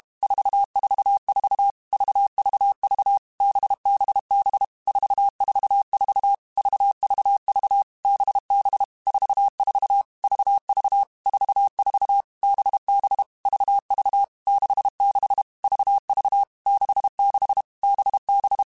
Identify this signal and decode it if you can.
Morse code, heard as a keyed tone, about 32 wpm, 800 Hz. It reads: 444 VVV BBB 444 VVV BB 44 VV 44 BB VV 66 VV 66 BB
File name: Groups@30 / Groups@30 mp3s / VB46@30